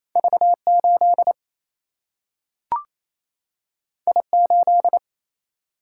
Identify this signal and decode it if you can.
Morse code, heard as a keyed tone, about 28 wpm, 700 Hz. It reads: V8 I8
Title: Single-Letter Single-Number 28wpm